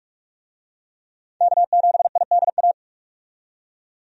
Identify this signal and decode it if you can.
Morse code, heard as a keyed tone, about 46 wpm, 700 Hz. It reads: K7IDA
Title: USA Call Signs - 5 Characters - 45wpm